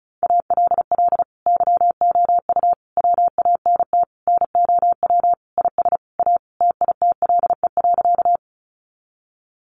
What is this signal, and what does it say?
Morse code, heard as a keyed tone, about 34 wpm, 700 Hz. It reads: ALL YOU WANT NOW IS A TITLE.
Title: Sentences from Top 1200 Words 35wpm